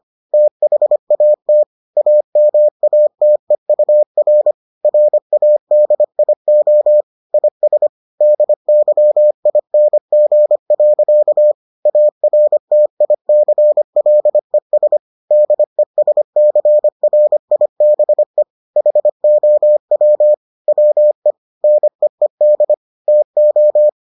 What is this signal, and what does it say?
Morse code, heard as a keyed tone, about 25 wpm, 600 Hz. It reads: THAT AMATEUR RADIO IS DYING. ARTICLES DESCRIBE HOW WE NEED TO